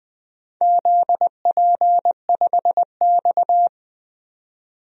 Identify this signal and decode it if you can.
Morse code, heard as a keyed tone, about 20 wpm, 700 Hz. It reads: ZP5X
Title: DXCC-100 Call Signs - 4 Characters - Repeated 2x - 20wpm